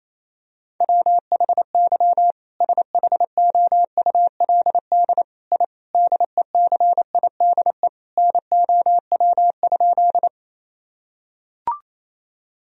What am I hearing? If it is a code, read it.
Morse code, heard as a keyed tone, about 28 wpm, 700 Hz. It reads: WHY SHOULD I DECIDE NOW?
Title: Sentences from Top 1000 Words - Review Part 3 - 28wpm